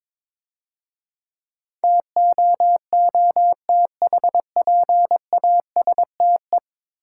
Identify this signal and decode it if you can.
Morse code, heard as a keyed tone, about 24 wpm, 690 Hz. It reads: TOOTHPASTE
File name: Easy Words - Speed-Racing with Hints - 22wpm